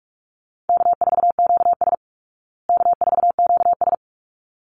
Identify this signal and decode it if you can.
Morse code, heard as a keyed tone, about 44 wpm, 700 Hz. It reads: K4QS K4QS
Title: Easy Call Signs Repeated 2x - 45wpm